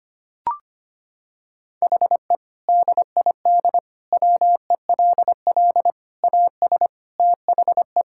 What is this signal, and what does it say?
Morse code, heard as a keyed tone, about 25 wpm, 700 Hz. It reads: HE DID WELL AS THE
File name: practice / 25wpm / Sentences with 3-Word Phrases - 25wpm